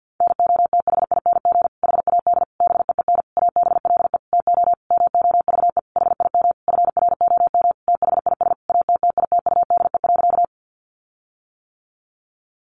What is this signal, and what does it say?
Morse code, heard as a keyed tone, about 50 wpm, 700 Hz. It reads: NOTHING HAD BEEN ABLE TO MOVE HIM FROM THIS ATTITUDE.
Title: Sentences from Top 1300 Words 50wpm